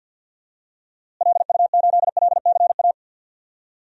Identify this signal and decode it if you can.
Morse code, heard as a keyed tone, about 47 wpm, 700 Hz. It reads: LU8LCA
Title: DXCC-100 Call Signs - 6 Characters - Repeated 2x - 50wpm